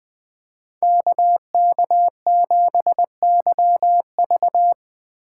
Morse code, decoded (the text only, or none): KK7YV